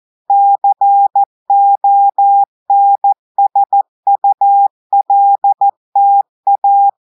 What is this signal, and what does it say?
Morse code, heard as a keyed tone, about 14 wpm, 800 Hz. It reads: CONSULTA
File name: audios / portugais / portugais texte en clair 1 vitesse 14 mots minutes